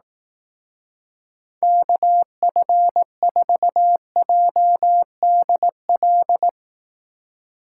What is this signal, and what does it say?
Morse code, heard as a keyed tone, about 18 wpm, 700 Hz. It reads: KF4JDL
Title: Speed Racing - USA 2x3 Call Signs 15wpm